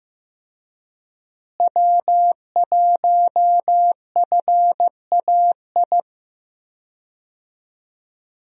Morse code, heard as a keyed tone, about 15 wpm, 700 Hz. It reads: W1FAI
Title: USA Call Signs - 5 Characters - Repeated 2x - 15wpm